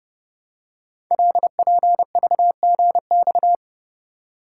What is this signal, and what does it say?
Morse code, heard as a keyed tone, about 28 wpm, 700 Hz. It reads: LPVGX
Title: Code Groups - 5 Letters - 30wpm